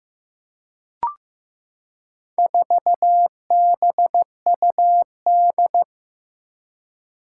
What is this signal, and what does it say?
Morse code, heard as a keyed tone, about 16 wpm, 700 Hz. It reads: 4BUD
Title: Four Challenging Letter-Number Combinations 15wpm